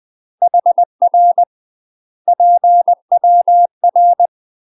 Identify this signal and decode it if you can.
Morse code, heard as a keyed tone, about 20 wpm, 700 Hz. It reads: HR PWR
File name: UK MASTER QSOs TRAINING / 20WPM / UK MASTER QSOs 0070